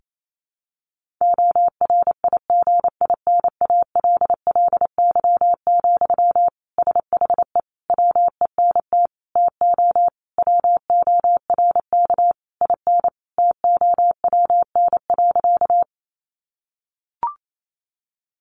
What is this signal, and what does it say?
Morse code, heard as a keyed tone, about 28 wpm, 700 Hz. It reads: ORIGINALLY, SHE WENT TO WORK IN TOWN.
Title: Sentences from Top 1500 Words 28wpm